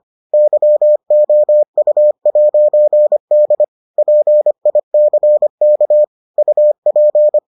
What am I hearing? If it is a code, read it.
Morse code, heard as a keyed tone, about 25 wpm, 600 Hz. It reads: YOU'D PICK UP